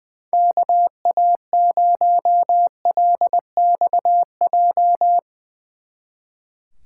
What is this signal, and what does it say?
Morse code, heard as a keyed tone, about 22 wpm, 700 Hz. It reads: KA0LXJ
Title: Speed Racing - USA 2x3 Call Signs 20wpm